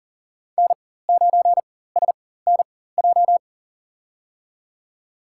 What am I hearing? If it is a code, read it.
Morse code, heard as a keyed tone, about 39 wpm, spaced as for 19 wpm, 700 Hz. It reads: N9SNJ
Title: USA Call Signs - 4x Character Spacing - 40wpm